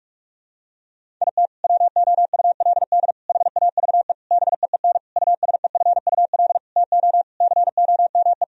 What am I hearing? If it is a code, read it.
Morse code, heard as a keyed tone, about 46 wpm, 700 Hz. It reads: IT WOULD HAVE BEEN USEFUL TO COME
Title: Sentences from Top 1000 Words - Review Part 4 - 45wpm